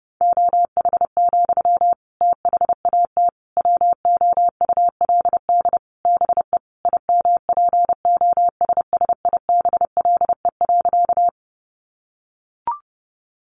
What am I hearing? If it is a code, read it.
Morse code, heard as a keyed tone, about 28 wpm, 700 Hz. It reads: OH, THAT WOULD BE IMPOSSIBLE.
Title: Sentences from Top 1100 Words 30wpm